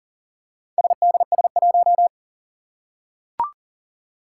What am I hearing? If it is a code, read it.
Morse code, heard as a keyed tone, about 42 wpm, 700 Hz. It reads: SDS1
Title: Four Challenging Letter-Number Combinations 40wpm